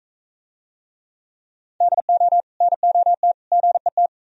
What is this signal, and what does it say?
Morse code, heard as a keyed tone, about 42 wpm, 700 Hz. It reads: DO NOT GET